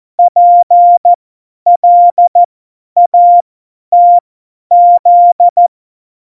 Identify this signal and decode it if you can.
Morse code, heard as a keyed tone, about 14 wpm, spaced as for 10 wpm, 700 Hz. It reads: PLATZ
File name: Morsekurs / Witze / Witz 25. Dezember, 50 F70